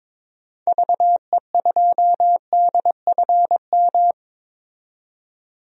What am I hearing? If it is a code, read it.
Morse code, heard as a keyed tone, about 22 wpm, 700 Hz. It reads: VE2DFM
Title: DXCC-100 Call Signs - 6 Characters - 22wpm